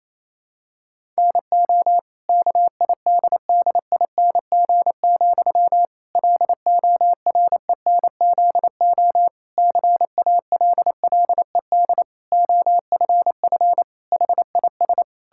Morse code, heard as a keyed tone, about 28 wpm, 700 Hz. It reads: NO KIDDING, LORENZO CALLED OFF HIS
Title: Pangram Challenge v2 - 28wpm